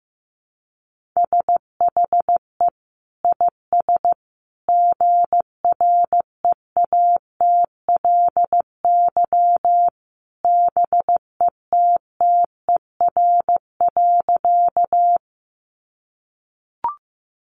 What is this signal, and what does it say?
Morse code, heard as a keyed tone, about 15 wpm, 700 Hz. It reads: SHE IS GREATLY BETTER.
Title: Sentences from Top 1400 Words 15wpm